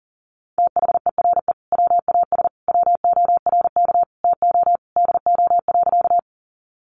Morse code, heard as a keyed tone, about 40 wpm, 700 Hz. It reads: THERE WAS WORK TO DO.
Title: Sentences from Top 200 Words 40wpm